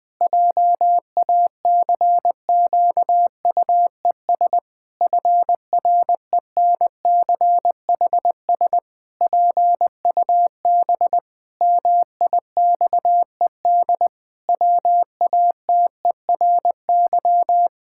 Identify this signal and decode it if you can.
Morse code, heard as a keyed tone, about 20 wpm, 700 Hz. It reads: JACQUES FRENCHS PUB MIXED WATERY
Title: Pangram Challenge v2 - 20wpm